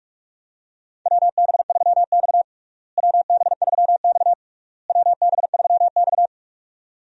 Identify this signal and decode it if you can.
Morse code, heard as a keyed tone, about 46 wpm, 700 Hz. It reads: WB3X WB3X WB3X
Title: USA Call Signs - 4 Characters - Repeated 3x - 45wpm